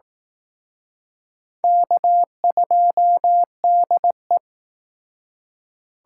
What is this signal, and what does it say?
Morse code, heard as a keyed tone, about 19 wpm, 700 Hz. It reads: K2DE